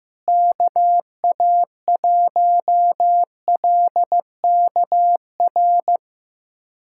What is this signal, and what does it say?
Morse code, heard as a keyed tone, about 15 wpm, 700 Hz. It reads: KA1LKR
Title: USA Callsign DE Callsign - 6 Characters - 15wpm